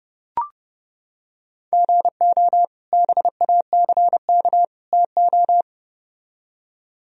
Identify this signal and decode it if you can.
Morse code, heard as a keyed tone, about 30 wpm, 700 Hz. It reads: GO BACK TO